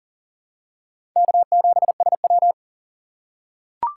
Morse code, heard as a keyed tone, about 40 wpm, 700 Hz. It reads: K7SW
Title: USA Call Signs - 4 Characters - 8x Character Spacing - 40wpm